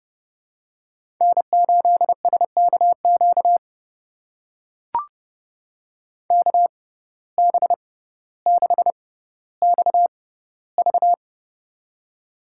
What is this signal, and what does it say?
Morse code, heard as a keyed tone, about 31 wpm, spaced as for 15 wpm, 700 Hz. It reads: N8SKQ KB6XV
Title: USA Call Signs - 5 Characters - 6x Character Spacing - 30wpm